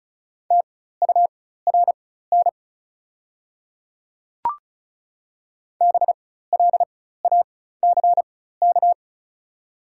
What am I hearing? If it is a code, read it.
Morse code, heard as a keyed tone, about 36 wpm, spaced as for 16 wpm, 690 Hz. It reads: TURN BLACK
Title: 4x Character Spacing - Top 500 Words - 35wpm